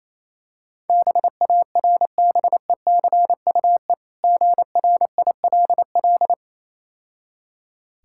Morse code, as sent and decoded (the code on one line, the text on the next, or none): -... .- .-. -... -.-. ..- . / --. .-. .. .-.. .-..
BARBECUE GRILL